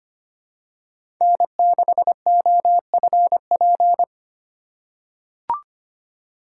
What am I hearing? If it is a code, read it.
Morse code, heard as a keyed tone, about 25 wpm, 700 Hz. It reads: N6OFP
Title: USA Call Signs - 5 Characters - 2x Character Spacing - 25wpm